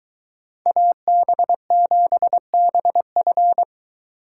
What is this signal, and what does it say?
Morse code, heard as a keyed tone, about 23 wpm, 700 Hz. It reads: AB7BF